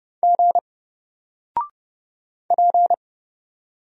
Morse code, heard as a keyed tone, about 30 wpm, 700 Hz. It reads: G P
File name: Single-Letter-Number-30wpm